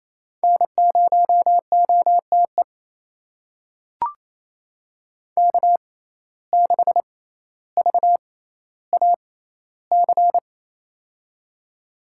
Morse code, heard as a keyed tone, about 28 wpm, spaced as for 14 wpm, 700 Hz. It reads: N0OTE K6VAC
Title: USA Call Signs - 5 Characters - 6x Character Spacing - 28wpm